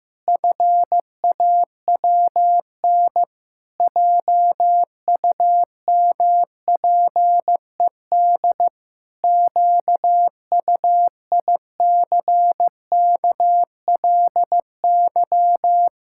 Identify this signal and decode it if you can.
Morse code, heard as a keyed tone, about 15 wpm, 700 Hz. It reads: FAWN JUMPED QUICKLY
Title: Pangram Challenge v2 - 15wpm